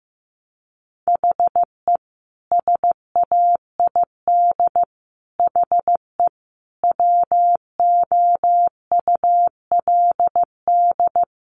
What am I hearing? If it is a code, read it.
Morse code, heard as a keyed tone, about 15 wpm, 700 Hz. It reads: HE SAID HE WOULD
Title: Sentences from Top 1200 Words 15wpm